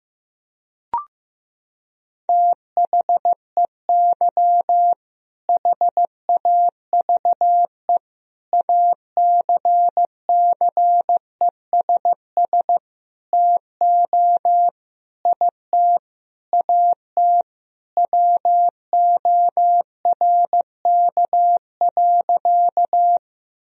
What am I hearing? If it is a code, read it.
Morse code, heard as a keyed tone, about 15 wpm, 700 Hz. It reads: THEY HAVE ACCESS TO IT AT WORK.
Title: Sentences from Top 1000 Words - Review Part 4 - 15wpm